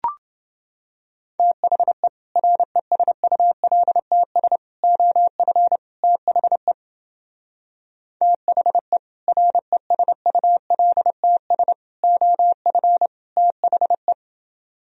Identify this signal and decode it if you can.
Morse code, heard as a keyed tone, about 28 wpm, 700 Hz. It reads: THE RESULTS OF THE THE RESULTS OF THE